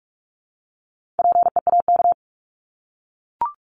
Morse code, heard as a keyed tone, about 44 wpm, 700 Hz. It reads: PEAK
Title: Top 1751-2000 Words 45wpm